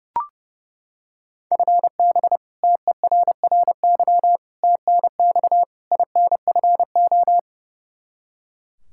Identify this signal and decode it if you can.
Morse code, heard as a keyed tone, about 30 wpm, 700 Hz. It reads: FB TERRY TNX INFO
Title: Practice QSO - Exchange 3 - More Info - 30wpm